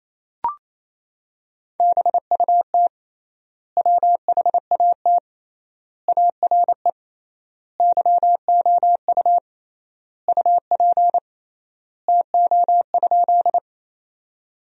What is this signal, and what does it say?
Morse code, heard as a keyed tone, about 28 wpm, 700 Hz. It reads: BUT WHAT ARE YOU UP TO?
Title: 3x Word-Spacing - Sentences from Top 100 Words - 28wpm